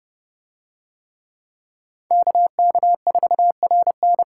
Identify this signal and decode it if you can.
Morse code, heard as a keyed tone, about 29 wpm, 700 Hz. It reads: KK4RN